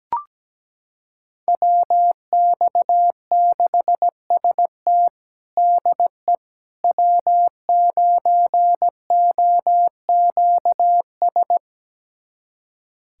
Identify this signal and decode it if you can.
Morse code, heard as a keyed tone, about 17 wpm, 700 Hz. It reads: WX6ST DE W9OQS
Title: USA Callsign DE Callsign - 5 Characters - 17wpm